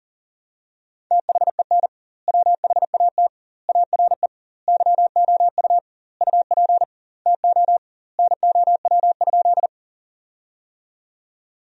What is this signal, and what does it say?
Morse code, heard as a keyed tone, about 40 wpm, 700 Hz. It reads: THEN WHAT ARE YOU UP TO NOW?
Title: 2x Word-Spacing - Sentences from Top 100 Words - 40wpm